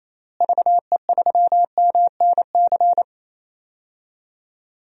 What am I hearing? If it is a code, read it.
Morse code, heard as a keyed tone, about 28 wpm, 700 Hz. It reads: VE3MNC
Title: DXCC-100 Call Signs - 6 Characters - Repeated 3x - 28wpm